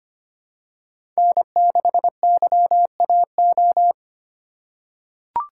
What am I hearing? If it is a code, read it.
Morse code, heard as a keyed tone, about 25 wpm, 700 Hz. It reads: N6YAO
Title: USA Call Signs - 5 Characters - 4x Character Spacing - 25wpm